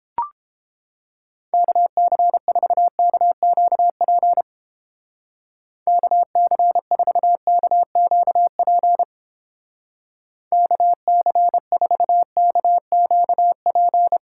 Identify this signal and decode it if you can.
Morse code, heard as a keyed tone, about 29 wpm, 700 Hz. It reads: KC4KQP KC4KQP KC4KQP